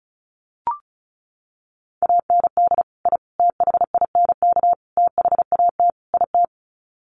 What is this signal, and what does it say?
Morse code, heard as a keyed tone, about 34 wpm, 700 Hz. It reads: AND I THINK THAT IT
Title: Sets of 5 Words Encore2 35wpm